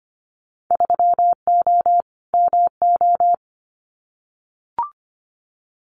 Morse code, heard as a keyed tone, about 25 wpm, 700 Hz. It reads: 3O MO
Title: Winter Field Day Exchanges Encore 25wpm